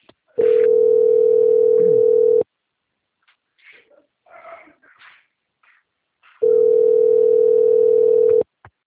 Ringback tone (North America), 440 + 480 Hz.